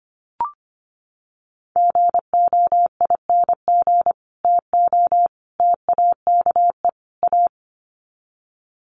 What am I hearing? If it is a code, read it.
Morse code, heard as a keyed tone, about 25 wpm, 700 Hz. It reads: GOING TO TAKE A
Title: Sets of 4 Words 25wpm